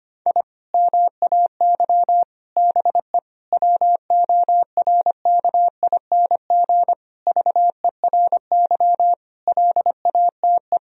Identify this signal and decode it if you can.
Morse code, heard as a keyed tone, about 25 wpm, 700 Hz. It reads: I MAY BE WORKING VERY LATE
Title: Sentences from Top 1000 Words - Review Part 4 - 25wpm